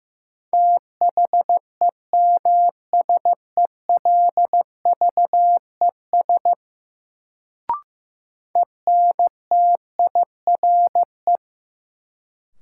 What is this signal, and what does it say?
Morse code, heard as a keyed tone, about 15 wpm, 700 Hz. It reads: THEMSELVES ENTIRE
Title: Sentences from Top 800 Words - New Word Drill - 15wpm